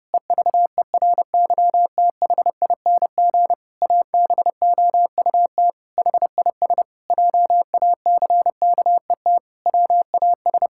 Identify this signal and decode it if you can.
Morse code, heard as a keyed tone, about 30 wpm, 700 Hz. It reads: EVERYTHING ABOUT HIS JACKET WAS
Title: Pangram Challenge v2 - 30wpm